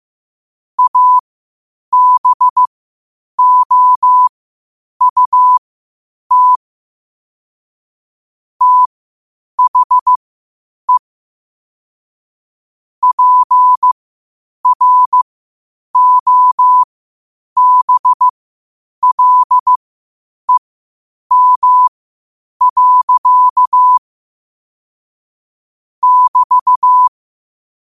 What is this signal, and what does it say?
Morse code, heard as a keyed tone, about 15 wpm, spaced as for 8 wpm, 1000 Hz. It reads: ABOUT THE PROBLEM. <BT>
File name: qotdfiles / 8wpm